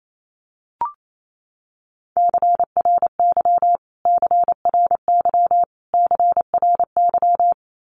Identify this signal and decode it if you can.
Morse code, heard as a keyed tone, about 28 wpm, 700 Hz. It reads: CRY CRY CRY
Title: Top 500 Words Repeated 28wpm